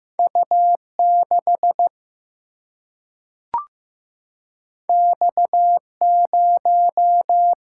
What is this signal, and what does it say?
Morse code, heard as a keyed tone, about 15 wpm, 700 Hz. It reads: U6 X0
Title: Single-Letter Single-Number 15wpm